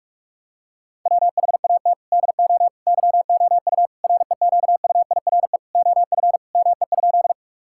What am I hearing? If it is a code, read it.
Morse code, heard as a keyed tone, about 41 wpm, 700 Hz. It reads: WHAT DO YOU REQUIRE OF ME?
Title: Sentences from Top 1000 Words - Review Part 6 - 45wpm